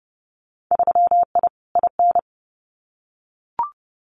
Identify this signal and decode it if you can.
Morse code, heard as a keyed tone, about 30 wpm, 700 Hz. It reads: 3I IN
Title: Winter Field Day Exchanges 30wpm